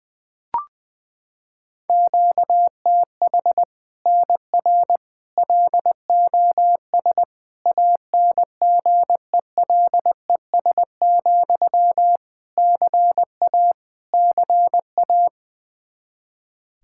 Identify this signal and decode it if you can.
Morse code, heard as a keyed tone, about 21 wpm, 700 Hz. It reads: QTH NR LOS ANGELES, CA CA
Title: Practice QSO - Exchange 1 - Beginning - 20wpm